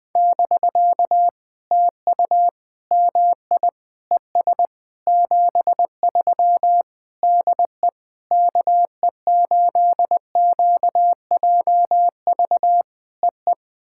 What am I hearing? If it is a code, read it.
Morse code, heard as a keyed tone, about 20 wpm, 700 Hz. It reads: <BK> TU MI ES 73 DE KE8QJV EE
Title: POTA QSO Practice - 20wpm